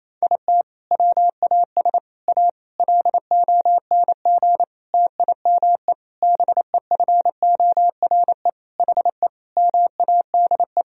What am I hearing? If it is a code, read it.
Morse code, heard as a keyed tone, about 28 wpm, 700 Hz. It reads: IT WAS A LONG TIME BEFORE HE MADE
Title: Sentences from Top 1000 Words - Review Part 2 - 28wpm